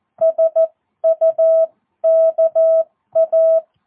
Morse code, heard as a keyed tone, about 14 wpm, spaced as for 12 wpm, 650 Hz. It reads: SUKA
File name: jh8cbh / 2022cw